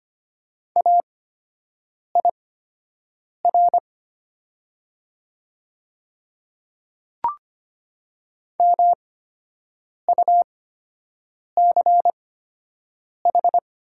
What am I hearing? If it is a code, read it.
Morse code, heard as a keyed tone, about 25 wpm, spaced as for 7 wpm, 700 Hz. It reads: AIR MUCH
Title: 8x Character Spacing - Top 300 Words - 25wpm